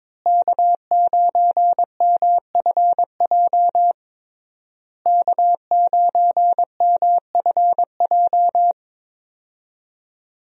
Morse code, heard as a keyed tone, about 22 wpm, 700 Hz. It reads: K9MFJ K9MFJ
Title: USA Call Signs - 5 Characters - Repeated 3x - 22wpm